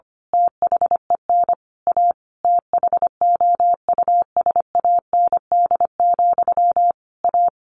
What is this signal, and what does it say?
Morse code, heard as a keyed tone, about 25 wpm, 700 Hz. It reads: THEN A THOUSAND, A
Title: A Sound Of Thunder 25wpm